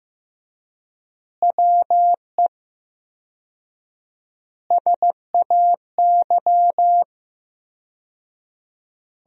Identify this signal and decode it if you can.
Morse code, heard as a keyed tone, about 15 wpm, 700 Hz. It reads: WE SAY
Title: 4x Word-Spacing - Sentences from Top 200 Words - 15wpm